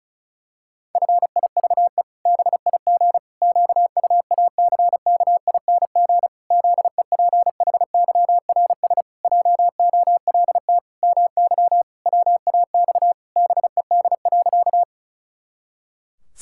Morse code, heard as a keyed tone, about 34 wpm, 700 Hz. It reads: FIVE BIG QUACKING ZEPHYRS JOLT MY WAX BED.
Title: Pangram Challenge v2 - 35wpm